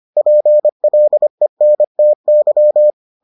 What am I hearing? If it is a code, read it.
Morse code, heard as a keyed tone, about 25 wpm, 600 Hz. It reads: PLENTY